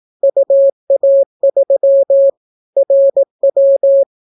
Morse code, heard as a keyed tone, about 18 wpm, 550 Hz. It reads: UA3 RW